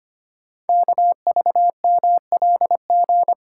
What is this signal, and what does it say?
Morse code, heard as a keyed tone, about 25 wpm, 700 Hz. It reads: KVMLG
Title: Code Groups - 5 Letters - 25wpm